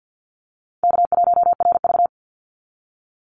Morse code, heard as a keyed tone, about 48 wpm, 700 Hz. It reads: K1RV
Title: Easy Call Signs Repeated 3x - 50wpm